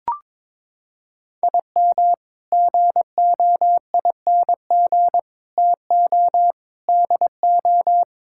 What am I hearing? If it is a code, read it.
Morse code, heard as a keyed tone, about 21 wpm, 700 Hz. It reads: IM GOING TO DO
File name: practice / 22wpm / Sets of 8 Words - 22wpm